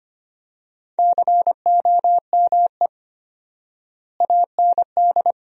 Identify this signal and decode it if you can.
Morse code, heard as a keyed tone, about 25 wpm, 700 Hz. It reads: COME AND
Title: 4x Word-Spacing - Sentences from Top 200 Words - 25wpm